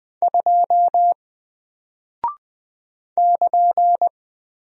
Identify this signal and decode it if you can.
Morse code, heard as a keyed tone, about 20 wpm, 700 Hz. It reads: 2 <KN>
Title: Single Character (Letter:Number:Punctuation:Prosign) Alt 20wpm